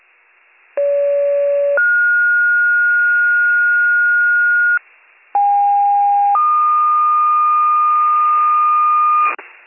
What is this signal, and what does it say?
Two-tone page, heard as a one-second tone, then a three-second tone, about 570 then 1430 Hz, about 810 then 1220 Hz, with noise. It is heard from 0.8 to 9.3 s.